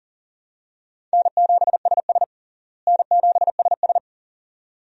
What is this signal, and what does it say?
Morse code, heard as a keyed tone, about 37 wpm, 700 Hz. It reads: N7SS N7SS
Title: USA Call Signs - 4 Characters - Repeated 2x - 40wpm